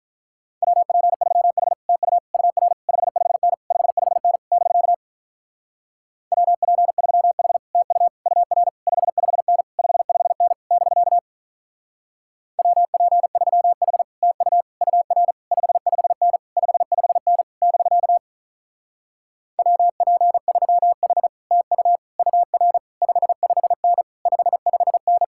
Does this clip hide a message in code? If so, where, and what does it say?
Yes: Morse code from 0.6 to 25.2 s, heard as a keyed tone, about 45 wpm, 700 Hz. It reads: WP3H TU UR 55N 55N <BK> WP3H TU UR 55N 55N <BK> WP3H TU UR 55N 55N <BK> WP3H TU UR 55N 55N